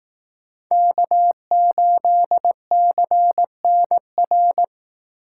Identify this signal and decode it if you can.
Morse code, heard as a keyed tone, about 18 wpm, 700 Hz. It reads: K8CNR
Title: Speed Racing - USA 1x3 Call Signs 15wpm